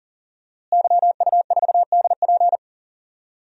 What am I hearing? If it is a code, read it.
Morse code, heard as a keyed tone, about 40 wpm, 700 Hz. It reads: YU4DP
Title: DXCC-100 Call Signs - 5 Characters - Repeated 2x - 40wpm